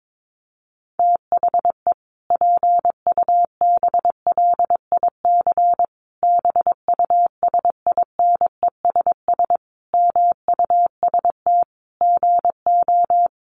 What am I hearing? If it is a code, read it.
Morse code, heard as a keyed tone, about 22 wpm, 700 Hz. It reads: THE PUBLIC BUSINESS MUST GO
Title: Sentences from Top 1300 Words 22wpm